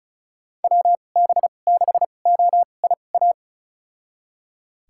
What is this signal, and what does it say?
Morse code, heard as a keyed tone, about 36 wpm, spaced as for 25 wpm, 700 Hz. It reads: WB6OIA
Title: USA Call Signs - 2x Character Spacing - 35wpm